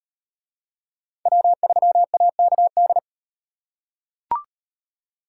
Morse code, heard as a keyed tone, about 33 wpm, 700 Hz. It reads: W3AKD